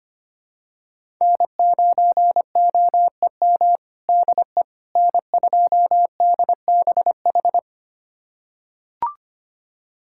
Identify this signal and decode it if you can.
Morse code, heard as a keyed tone, about 25 wpm, 700 Hz. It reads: N9OEM DE N2DBH
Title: USA Callsign DE Callsign - 5 Characters - 25wpm